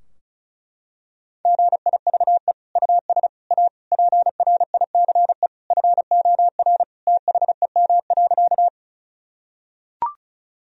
Morse code, heard as a keyed tone, about 36 wpm, 700 Hz. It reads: GIVE US A PRICE FOR THEM.
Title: Sentences from Top 1000 Words - Review Part 4 - 35wpm